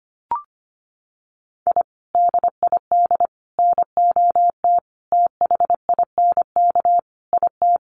Morse code, heard as a keyed tone, about 25 wpm, 700 Hz. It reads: I DID NOT THINK IT
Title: Sets of 5 Words Encore2 25wpm